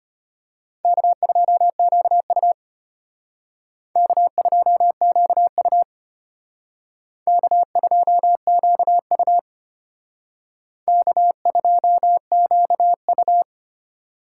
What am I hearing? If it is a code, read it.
Morse code, heard as a keyed tone, about 31 wpm, 700 Hz. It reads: K2QU K2QU K2QU K2QU